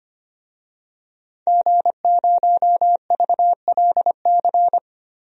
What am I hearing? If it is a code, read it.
Morse code, heard as a keyed tone, about 25 wpm, 700 Hz. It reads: G0VLC
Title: DXCC-100 Call Signs - 5 Characters - Repeated 3x - 25wpm